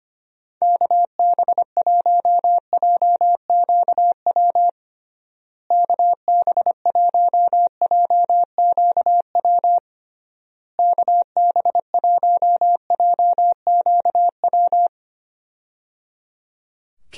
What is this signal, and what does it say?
Morse code, heard as a keyed tone, about 25 wpm, 700 Hz. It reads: KB1JQW KB1JQW KB1JQW
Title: USA Call Signs - 6 Characters - Repeated 3x - 25wpm